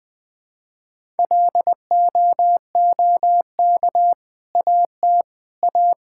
Morse code, heard as a keyed tone, about 20 wpm, 700 Hz. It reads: LOOK AT A
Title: Sentences with 2-Word Phrases - Spoken Beginning - 20wpm